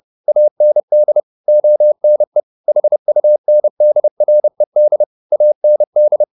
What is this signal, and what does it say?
Morse code, heard as a keyed tone, about 30 wpm, 600 Hz. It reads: AND ONE HUNDRED AND
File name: the old man and the sea / the old man and the sea 1 30wpm